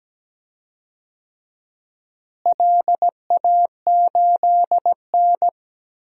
Morse code, heard as a keyed tone, about 17 wpm, 700 Hz. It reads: LA8N
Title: DXCC-100 Call Signs - 4 Characters - Repeated 3x - 17wpm